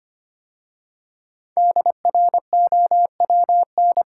Morse code, heard as a keyed tone, about 25 wpm, 700 Hz. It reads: DROWN
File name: 5 Letter Words - First 3 Letters Given - 25wpm